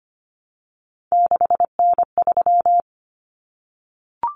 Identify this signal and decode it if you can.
Morse code, heard as a keyed tone, about 25 wpm, 700 Hz. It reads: 6N3
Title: Common Cut Numbers - 3-Digit 25wpm